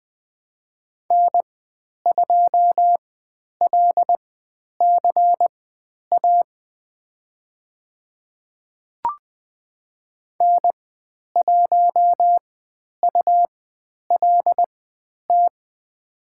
Morse code, heard as a keyed tone, about 20 wpm, spaced as for 10 wpm, 700 Hz. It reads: N2LCA N1ULT
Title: 109 Lesson 18g - Letter m - Call Signs 20wpm